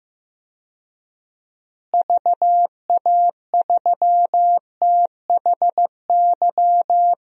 Morse code, heard as a keyed tone, about 15 wpm, 700 Hz. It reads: VA3THY
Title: DXCC-100 Call Signs - 6 Characters - Repeated 3x - 15wpm